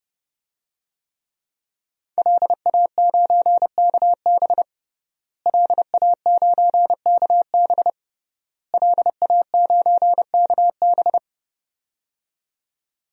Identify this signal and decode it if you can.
Morse code, heard as a keyed tone, about 30 wpm, 700 Hz. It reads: LA9KB LA9KB LA9KB
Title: DXCC-100 Call Signs - 5 Characters - Repeated 3x - 30wpm